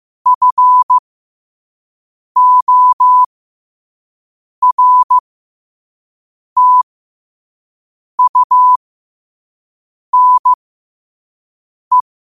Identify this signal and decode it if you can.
Morse code, heard as a keyed tone, about 15 wpm, spaced as for 5 wpm, 1000 Hz. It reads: FORTUNE
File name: qotdfiles / 5wpm